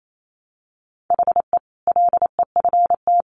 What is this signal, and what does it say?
Morse code, heard as a keyed tone, about 28 wpm, 700 Hz. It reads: HE LEFT
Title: Sentences from Top 1100 Words 28wpm